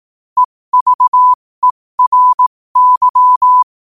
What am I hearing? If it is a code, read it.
Morse code, heard as a keyed tone, about 18 wpm, spaced as for 15 wpm, 1000 Hz. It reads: EVERY
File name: qotdfiles / 15wpm